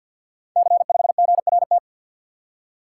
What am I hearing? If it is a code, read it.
Morse code, heard as a keyed tone, about 51 wpm, 700 Hz. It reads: C5GRT
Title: Code Groups - 5 Letter:Numbers - 50wpm